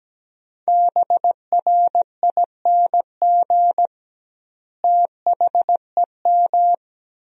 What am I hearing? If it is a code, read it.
Morse code, heard as a keyed tone, about 17 wpm, 700 Hz. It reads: BRING THEM
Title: 2x Word-Spacing - Sentences from Top 500 Words - 17wpm